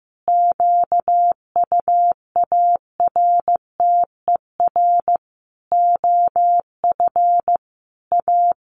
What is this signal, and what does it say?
Morse code, heard as a keyed tone, about 15 wpm, 700 Hz. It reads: QUARTER OF A
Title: Sentences from Top 1300 Words 15wpm